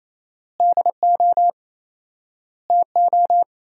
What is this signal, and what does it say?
Morse code, heard as a keyed tone, about 28 wpm, 700 Hz. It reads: DO TO
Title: 4x Word-Spacing - Sentences from Top 100 Words - 28wpm